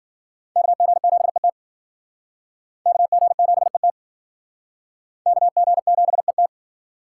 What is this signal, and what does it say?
Morse code, heard as a keyed tone, about 50 wpm, 700 Hz. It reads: KG7ET KG7ET KG7ET